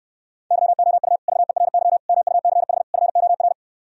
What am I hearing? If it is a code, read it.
Morse code, heard as a keyed tone, about 67 wpm, 700 Hz. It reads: YOU VERY MUCH FOR